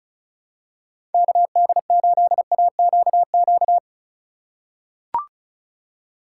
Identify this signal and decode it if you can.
Morse code, heard as a keyed tone, about 36 wpm, 700 Hz. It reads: KD8AQQ
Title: USA Call Signs - 6 Characters - 3x Character Spacing - 35wpm